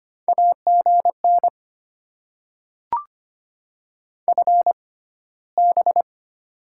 Morse code, heard as a keyed tone, about 25 wpm, spaced as for 15 wpm, 700 Hz. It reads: AGN FB
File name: Top 20 QSO Elements - 6x Character Spacing - 25wpm